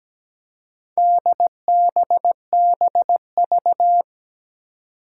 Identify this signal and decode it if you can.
Morse code, heard as a keyed tone, about 17 wpm, 700 Hz. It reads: DBBV
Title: Four Challenging Letter-Number Combinations 17wpm